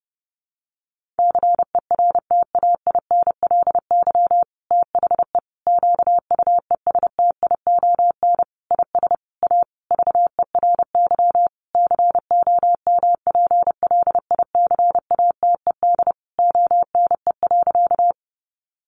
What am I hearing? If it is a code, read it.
Morse code, heard as a keyed tone, about 30 wpm, 700 Hz. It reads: CERTAINLY THE QUESTION IS A VERY COMPLICATED ONE.
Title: Sentences from Top 1200 Words 30wpm